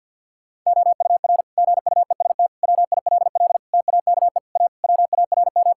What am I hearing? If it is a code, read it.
Morse code, heard as a keyed tone, about 52 wpm, 700 Hz. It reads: OUR GUEST WILL TAKE A WARM
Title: Sentences from Top 1600 Words 50wpm